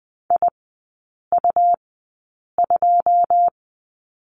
Morse code, heard as a keyed tone, about 20 wpm, spaced as for 8 wpm, 700 Hz. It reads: IU2
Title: 100 Lesson 17e - Letter u - Three Characters 20wpm